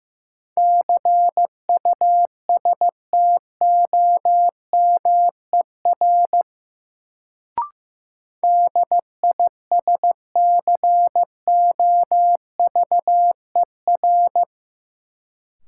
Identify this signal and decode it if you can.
Morse code, heard as a keyed tone, about 16 wpm, 700 Hz. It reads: CUSTOMER DISCOVER
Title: Sentences from Top 1100 Words - New Word Drill Alt - 15wpm